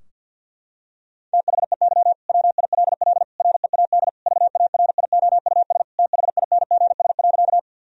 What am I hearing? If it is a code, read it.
Morse code, heard as a keyed tone, about 50 wpm, 700 Hz. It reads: THEY WILL READ VARIOUS THINGS.
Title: Sentences from Top 1000 Words - Review Part 4 - 50wpm